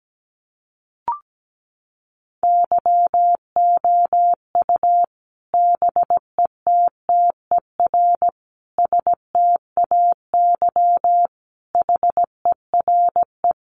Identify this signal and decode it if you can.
Morse code, heard as a keyed tone, about 17 wpm, 700 Hz. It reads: YOU BETTER STAY HERE
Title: Sentences from Top 500 Words 17wpm